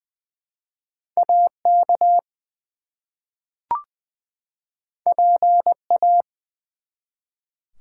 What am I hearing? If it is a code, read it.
Morse code, heard as a keyed tone, about 20 wpm, 700 Hz. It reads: AK PA
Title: US States and Canadian Provinces 20wpm